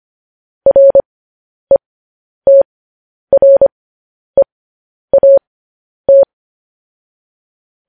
Morse code, heard as a keyed tone, about 25 wpm, spaced as for 10 wpm, 550 Hz. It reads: RETREAT
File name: podcast / News headlines-1x-25x10